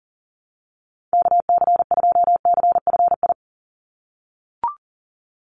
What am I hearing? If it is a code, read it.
Morse code, heard as a keyed tone, about 41 wpm, 700 Hz. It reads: KC2CFI
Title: USA Call Signs 40wpm